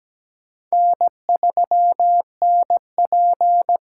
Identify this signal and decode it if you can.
Morse code, heard as a keyed tone, about 17 wpm, 700 Hz. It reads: N3NP
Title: USA Call Signs - 4 Characters - 4x Character Spacing - 17wpm